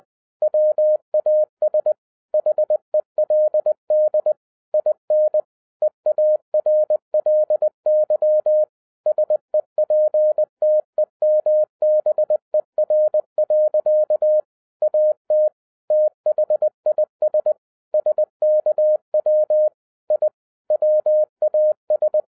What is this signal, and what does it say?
Morse code, heard as a keyed tone, about 20 wpm, 610 Hz. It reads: WAS HELD IN EARLY SEPTEMBER. AT THIS SKW I WAS